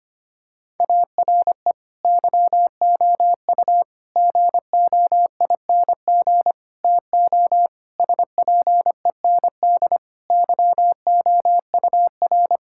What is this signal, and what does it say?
Morse code, heard as a keyed tone, about 25 wpm, 700 Hz. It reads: ARE YOU GOING TO SPEND YOUR
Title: Sentences from Top 1000 Words - Review Part 5 - 25wpm